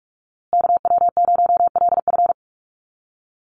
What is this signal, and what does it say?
Morse code, heard as a keyed tone, about 44 wpm, 700 Hz. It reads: KW0LF
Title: USA Call Signs Repeated 3x 45wpm